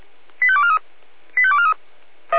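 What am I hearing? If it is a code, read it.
Five-tone selective call (ZVEI), heard as quick runs of five tones, tones of about 70 ms. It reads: 85243 85243
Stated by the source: ZVEI five-tone selective call received on SDR.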